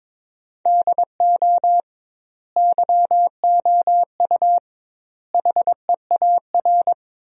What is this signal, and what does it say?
Morse code, heard as a keyed tone, about 22 wpm, 700 Hz. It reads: DO YOU HEAR
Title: 2x Word-Spacing - Sentences from Top 300 Words - 22wpm